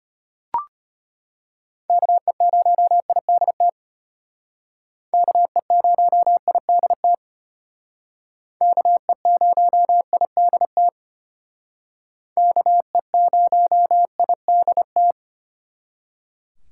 Morse code, heard as a keyed tone, about 33 wpm, 700 Hz. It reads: KE0IDT KE0IDT KE0IDT KE0IDT